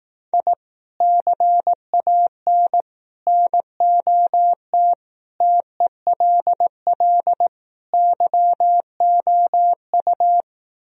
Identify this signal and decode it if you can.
Morse code, heard as a keyed tone, about 18 wpm, 700 Hz. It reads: I CAN NOT TELL YOU
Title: Speed-Racing - Sets of 5 Words 15wpm